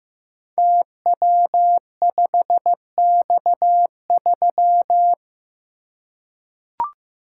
Morse code, heard as a keyed tone, about 15 wpm, 700 Hz. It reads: TW5X3
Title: Code Groups - 5 Letter:Numbers - 15wpm